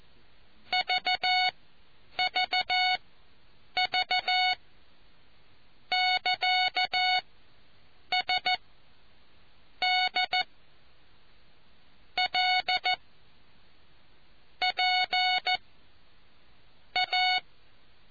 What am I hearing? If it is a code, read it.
Morse code, heard as a keyed tone, about 14 wpm, spaced as for 6 wpm, 2200 Hz. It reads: VVV <KA> SDLPA